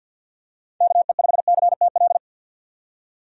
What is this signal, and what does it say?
Morse code, heard as a keyed tone, about 52 wpm, 700 Hz. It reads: KE5CTL
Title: USA Call Signs - 6 Characters - 50wpm